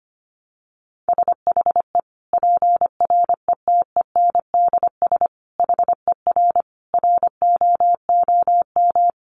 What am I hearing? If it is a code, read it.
Morse code, heard as a keyed tone, about 25 wpm, 700 Hz. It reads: SHE PRETENDS HER ROOM
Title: Additional Easy Phrases 25wpm